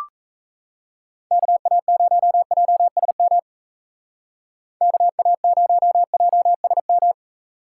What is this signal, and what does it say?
Morse code, heard as a keyed tone, about 34 wpm, 700 Hz. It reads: KA0JSM KA0JSM
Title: Speed Racing - USA 2x3 Call Signs 28wpm